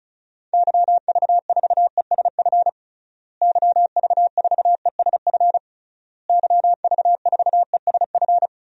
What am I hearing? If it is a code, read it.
Morse code, heard as a keyed tone, about 34 wpm, 700 Hz. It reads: YV4ESF YV4ESF YV4ESF
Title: DXCC-100 Call Signs - 6 Characters - Repeated 3x - 35wpm